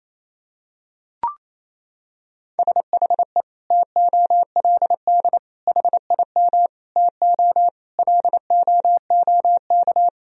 Morse code, heard as a keyed tone, about 28 wpm, 700 Hz. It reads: SHE TOLD HIM TO LOOK
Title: Sentences from Top 1000 Words - Review Part 1 - 28wpm